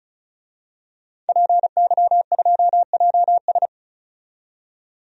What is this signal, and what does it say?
Morse code, heard as a keyed tone, about 36 wpm, 700 Hz. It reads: PY2JS
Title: DXCC-100 Call Signs - 5 Characters - 35wpm